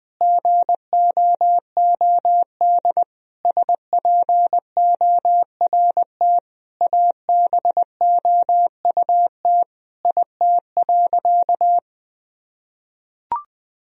Morse code, heard as a keyed tone, about 20 wpm, 700 Hz. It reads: GOOD SPORT ABOUT IT.
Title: Sentences from Top 1000 Words - Review Part 4 - 20wpm